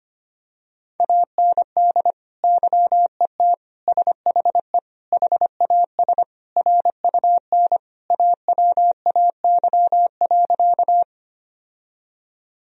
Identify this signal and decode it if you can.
Morse code, heard as a keyed tone, about 25 wpm, 700 Hz. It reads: AND YET SHE HAS RUN AWAY.
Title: Sentences from Top 1000 Words - Review Part 1 - 25wpm